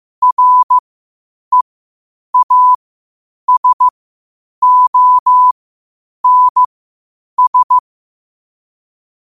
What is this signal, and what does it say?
Morse code, heard as a keyed tone, about 15 wpm, spaced as for 8 wpm, 1000 Hz. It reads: REASONS
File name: qotdfiles / 8wpm